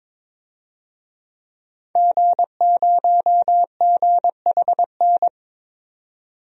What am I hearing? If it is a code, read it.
Morse code, heard as a keyed tone, about 22 wpm, 700 Hz. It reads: G0GHN
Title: DXCC-100 Call Signs - 5 Characters - 22wpm